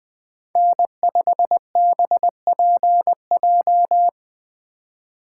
Morse code, heard as a keyed tone, about 20 wpm, 700 Hz. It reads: N5BPJ
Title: USA Call Signs - 5 Characters - 2x Character Spacing - 20wpm